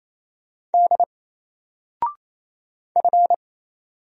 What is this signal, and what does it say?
Morse code, heard as a keyed tone, about 27 wpm, 700 Hz. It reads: D F
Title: Single Letter-Number Alt 28wpm